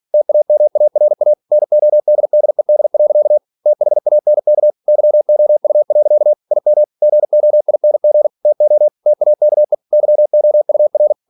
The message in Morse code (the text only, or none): TAMARA NODDED. THANK YOU. IM GOING TO TAKE YOUR